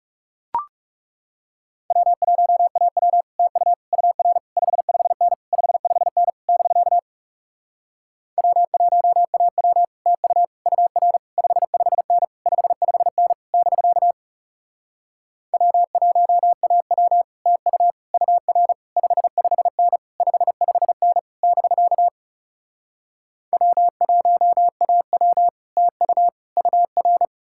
Morse code, heard as a keyed tone, about 40 wpm, 700 Hz. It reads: W1AW TU UR 55N 55N <BK> W1AW TU UR 55N 55N <BK> W1AW TU UR 55N 55N <BK> W1AW TU UR